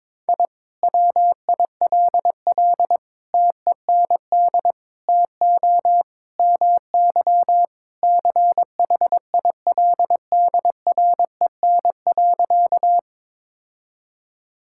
Morse code, heard as a keyed tone, about 22 wpm, 700 Hz. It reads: I WILL TEND TO MY CHILDREN.
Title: Sentences from Top 1000 Words - Review Part 5 - 22wpm